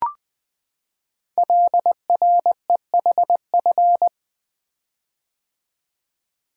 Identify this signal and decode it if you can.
Morse code, heard as a keyed tone, about 20 wpm, 700 Hz. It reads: LREHF
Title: Code Groups - 5 Letters - 20wpm